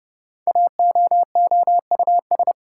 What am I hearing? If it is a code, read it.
Morse code, heard as a keyed tone, about 30 wpm, 700 Hz. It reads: AOOUS
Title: Code Groups - 5 Letters - 30wpm